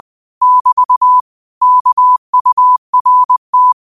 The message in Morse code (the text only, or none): <BT> KURT